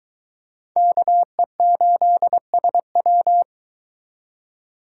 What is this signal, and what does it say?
Morse code, heard as a keyed tone, about 22 wpm, 700 Hz. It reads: KE8SW